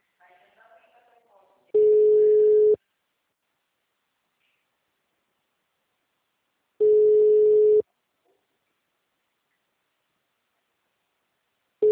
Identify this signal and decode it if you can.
Ringback tone (Europe, 425 Hz).